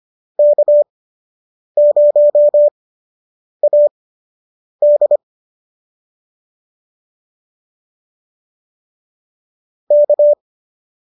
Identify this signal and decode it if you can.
Morse code, heard as a keyed tone, about 25 wpm, spaced as for 8 wpm, 570 Hz. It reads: K0AD K